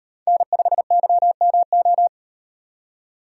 Morse code, heard as a keyed tone, about 36 wpm, 700 Hz. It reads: N5YMO